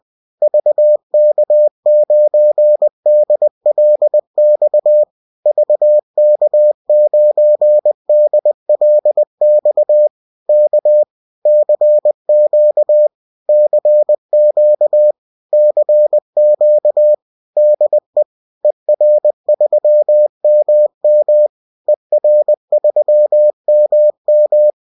Morse code, heard as a keyed tone, about 20 wpm, 600 Hz. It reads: VK9DLX VK9DLX K CQ CQ CQ DE ER3MM ER3MM